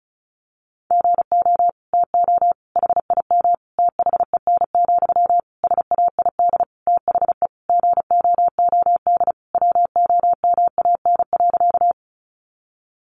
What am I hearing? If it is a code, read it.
Morse code, heard as a keyed tone, about 34 wpm, 700 Hz. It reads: GO TO HIM THEN, SAID THE GOOD WOMAN.
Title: Sentences from Top 1100 Words 35wpm